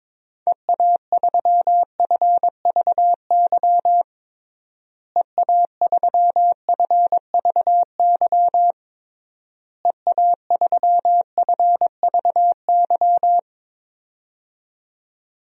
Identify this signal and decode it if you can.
Morse code, heard as a keyed tone, about 22 wpm, 700 Hz. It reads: EA3FVY EA3FVY EA3FVY
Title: DXCC-100 Call Signs - 6 Characters - Repeated 3x - 22wpm